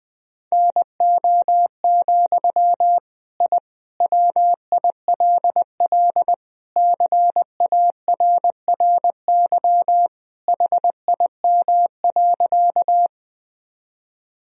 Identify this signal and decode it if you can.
Morse code, heard as a keyed tone, about 20 wpm, 700 Hz. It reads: NO, I WILL CARRY HIM.
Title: Sentences from Top 1000 Words - Review Part 1 - 20wpm